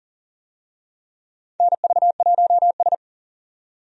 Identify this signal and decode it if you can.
Morse code, heard as a keyed tone, about 40 wpm, 700 Hz. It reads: NV1S
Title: USA Call Signs - 4 Characters - 40wpm